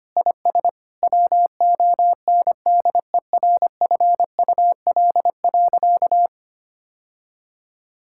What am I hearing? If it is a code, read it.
Morse code, heard as a keyed tone, about 25 wpm, 700 Hz. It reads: IS WONDERFUL.
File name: practice / 25wpm / Sentences with 3-Word Phrases - 25wpm